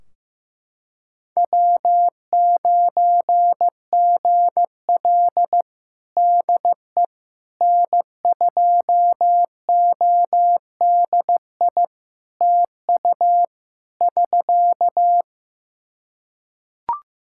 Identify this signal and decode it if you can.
Morse code, heard as a keyed tone, about 15 wpm, 700 Hz. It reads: W9GL DE N2ODI TU <SK>
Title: Practice QSO - Exchange 4 - Ending - 15wpm